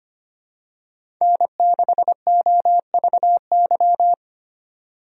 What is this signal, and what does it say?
Morse code, heard as a keyed tone, about 25 wpm, 700 Hz. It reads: N6OVY
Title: USA Call Signs - 5 Characters - 25wpm